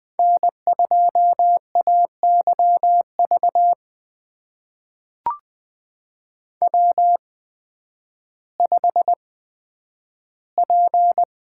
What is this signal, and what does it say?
Morse code, heard as a keyed tone, about 20 wpm, spaced as for 11 wpm, 700 Hz. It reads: N2AYV W5P
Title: USA Call Signs - 5 Characters - 8x Character Spacing - 20wpm